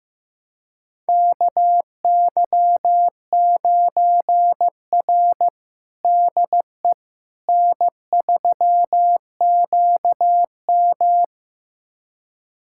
Morse code, heard as a keyed tone, about 15 wpm, 700 Hz. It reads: KY9R DE N3QM
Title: USA Callsign DE Callsign - 4 Characters - 15wpm